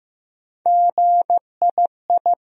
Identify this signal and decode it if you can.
Morse code, heard as a keyed tone, about 15 wpm, 700 Hz. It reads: GII
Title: USA Call Sign Postfixes - 3 Letters 15wpm